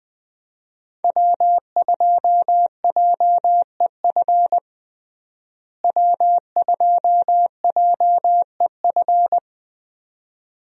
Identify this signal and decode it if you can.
Morse code, heard as a keyed tone, about 20 wpm, 700 Hz. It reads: W2JEF W2JEF
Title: USA Call Signs - 5 Characters - Repeated 2x - 20wpm